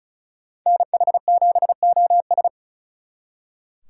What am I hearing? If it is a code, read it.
Morse code, heard as a keyed tone, about 42 wpm, 700 Hz. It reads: NH7OS
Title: Speed Racing - USA 2x2 Call Signs 35wpm